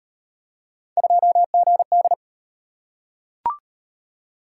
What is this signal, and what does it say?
Morse code, heard as a keyed tone, about 35 wpm, 700 Hz. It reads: 2GD